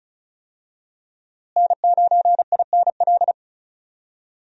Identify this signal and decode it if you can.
Morse code, heard as a keyed tone, about 34 wpm, 700 Hz. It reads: N9INL
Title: USA Call Signs - 5 Characters - Repeated 3x - 35wpm